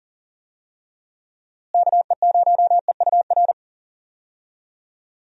Morse code, heard as a keyed tone, about 40 wpm, 700 Hz. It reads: KE0EUR
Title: USA Call Signs - 6 Characters - Repeated 2x - 40wpm